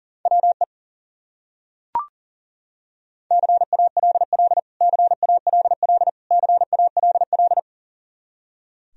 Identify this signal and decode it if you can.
Morse code, heard as a keyed tone, about 37 wpm, 700 Hz. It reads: WE CALL CALL CALL
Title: Top 100 Words Repeated 40wpm